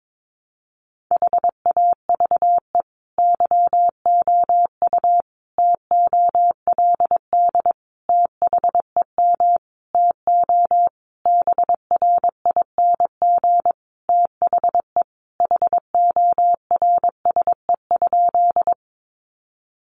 Morse code, heard as a keyed tone, about 22 wpm, 700 Hz. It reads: HAVE YOU TOLD THEM TO BRING THE HORSE?
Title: Sentences from Top 500 Words 22wpm